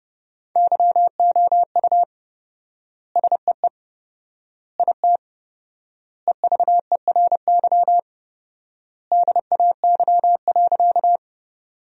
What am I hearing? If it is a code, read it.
Morse code, heard as a keyed tone, about 30 wpm, 700 Hz. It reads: YOU SEE IT EVERY DAY.
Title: 4x Word-Spacing - Sentences from Top 200 Words - 30wpm